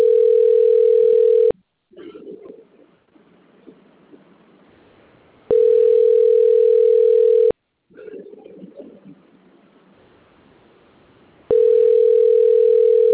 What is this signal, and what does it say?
Ringback tone (North America), 440 + 480 Hz.
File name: out- - - - - .11816